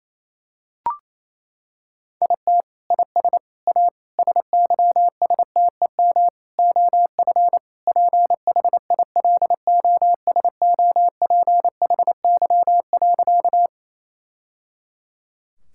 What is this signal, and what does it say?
Morse code, heard as a keyed tone, about 28 wpm, 700 Hz. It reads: IT IS A SYSTEM OF PHILOSOPHY.
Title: Sentences from Top 1000 Words - Review Part 6 - 28wpm